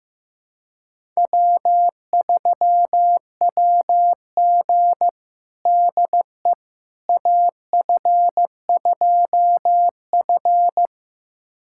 Morse code, heard as a keyed tone, about 16 wpm, 700 Hz. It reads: W3WG DE AF2F
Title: USA Callsign DE Callsign - 4 Characters - 15wpm